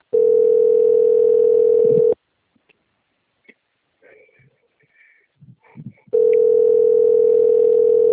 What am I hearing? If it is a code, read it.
Ringback tone (North America), 440 + 480 Hz.